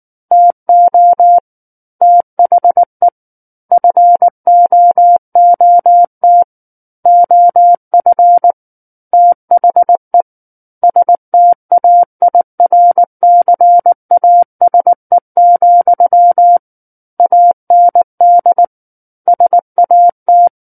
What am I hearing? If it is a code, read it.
Morse code, heard as a keyed tone, about 19 wpm, 700 Hz. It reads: TO THE FOOT OF THE STAIRCASE, AND SAT
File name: Book One / 10-Chapter 10 - 19 WPM